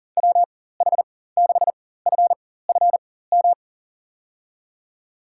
Morse code, heard as a keyed tone, about 40 wpm, spaced as for 19 wpm, 700 Hz. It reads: WH6FFM
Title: USA Call Signs - 6 Characters - 4x Character Spacing - 40wpm